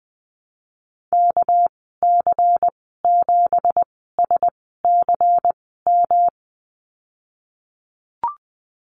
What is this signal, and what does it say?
Morse code, heard as a keyed tone, about 20 wpm, spaced as for 15 wpm, 700 Hz. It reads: KC7SCM
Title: 213 Lesson 34g - Number 8 - Call Signs 20wpm